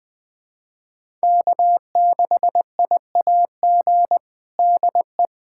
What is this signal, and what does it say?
Morse code, heard as a keyed tone, about 20 wpm, 700 Hz. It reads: K6IAG DE
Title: USA Callsign DE Callsign - 5 Characters - 20wpm